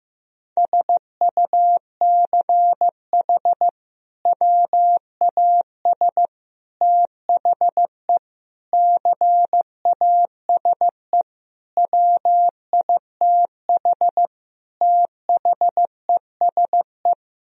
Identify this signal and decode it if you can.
Morse code, heard as a keyed tone, about 15 wpm, 700 Hz. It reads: SUCH WAS THE CASE WITH THESE
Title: Sentences from Top 1000 Words - Review Part 1 - 15wpm